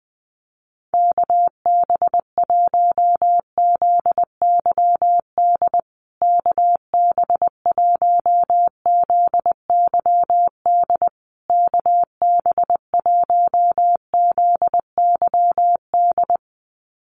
Morse code, heard as a keyed tone, about 20 wpm, 700 Hz. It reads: KB1ZYD KB1ZYD KB1ZYD
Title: USA Call Signs Repeated 3x 20wpm